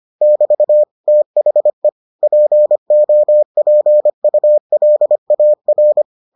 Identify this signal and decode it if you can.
Morse code, heard as a keyed tone, about 25 wpm, 600 Hz. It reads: <BT> THE POPULAR